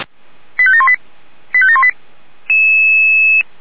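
Five-tone selective call (ZVEI), heard as quick runs of five tones, tones of about 70 ms. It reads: 85718 85718
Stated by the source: ZVEI five-tone call received by SDR, from feuerwehr-orlamuende.org fire brigade audio server.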